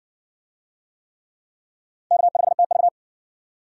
Morse code, heard as a keyed tone, about 51 wpm, 700 Hz. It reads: K5TV